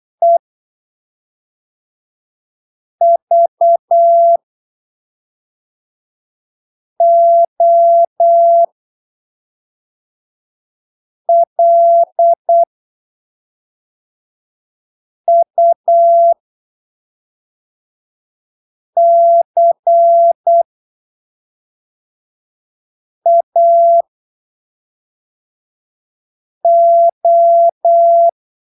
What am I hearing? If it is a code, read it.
Morse code, heard as a keyed tone, about 8 wpm, spaced as for 3 wpm, 680 Hz. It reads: EVOLUCAO
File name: arquivos / texto / texto 05 08ppm